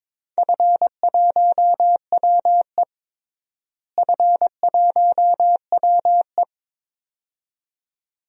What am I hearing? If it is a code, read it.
Morse code, heard as a keyed tone, about 22 wpm, 700 Hz. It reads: F1WE F1WE
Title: DXCC-100 Call Signs - 4 Characters - Repeated 2x - 22wpm